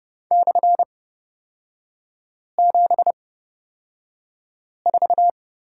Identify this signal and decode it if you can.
Morse code, heard as a keyed tone, about 30 wpm, spaced as for 5 wpm, 700 Hz. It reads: /74
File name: Lesson 37e - Letter x - Three Characters 30wpm